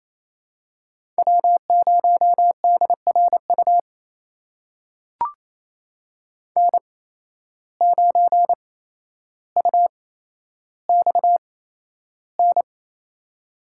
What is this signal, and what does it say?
Morse code, heard as a keyed tone, about 28 wpm, spaced as for 12 wpm, 700 Hz. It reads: W0DRU N9UXN
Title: USA Call Signs - 8x Character Spacing - 28wpm